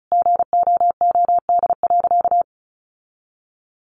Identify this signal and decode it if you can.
Morse code, heard as a keyed tone, about 34 wpm, 700 Hz. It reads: GOOD.
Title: Sentences from Top 1400 Words 35wpm